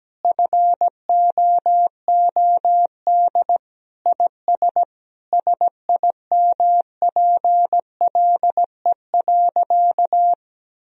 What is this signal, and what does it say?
Morse code, heard as a keyed tone, about 17 wpm, 700 Hz. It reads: FOOD IS SIMPLE.
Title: Sentences from Top 1000 Words - Review Part 1 - 17wpm